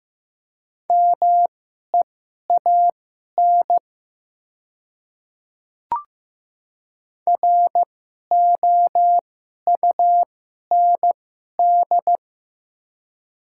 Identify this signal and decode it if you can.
Morse code, heard as a keyed tone, about 15 wpm, spaced as for 11 wpm, 700 Hz. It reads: MEAN ROUND
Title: 2x Character Spacing - Top 200 Words - 15wpm